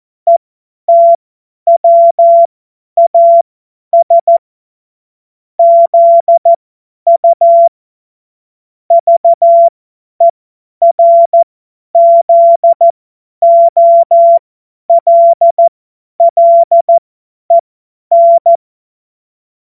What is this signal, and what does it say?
Morse code, heard as a keyed tone, about 14 wpm, spaced as for 10 wpm, 680 Hz. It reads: ETWAS ZU VERZOLLEN
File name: Morsekurs / Witze / Witz 07.November 2023 50 F70